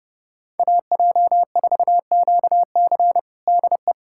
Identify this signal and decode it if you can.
Morse code, heard as a keyed tone, about 32 wpm, 700 Hz. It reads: AJ4QC DE
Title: USA Callsign DE Callsign - 5 Characters - 30wpm